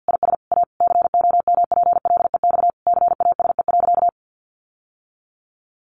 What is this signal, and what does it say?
Morse code, heard as a keyed tone, about 50 wpm, 700 Hz. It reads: IS A COMPLEX CASE.
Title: Sentences from Top 900 Words 50wpm